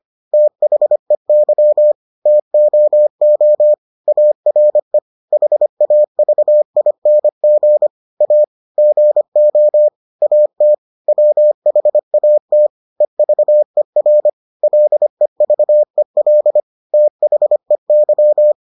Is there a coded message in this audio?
Yes: Morse code from 0.3 to 18.5 s, heard as a keyed tone, about 25 wpm, 600 Hz. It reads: THEY TOO ARE HAVING A GO AT WHAT EVER LEVEL THEY